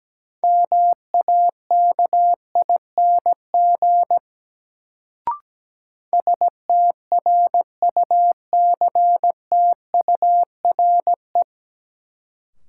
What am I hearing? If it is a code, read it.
Morse code, heard as a keyed tone, about 17 wpm, 700 Hz. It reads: MAKING STRUCTURE
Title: Sentences from Top 1000 Words - Review Part 6 - New Word Drill - 17wpm